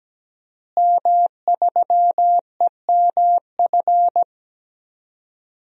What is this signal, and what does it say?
Morse code, heard as a keyed tone, about 17 wpm, 700 Hz. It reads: M3EMF
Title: DXCC-100 Call Signs - 5 Characters - Repeated 3x - 17wpm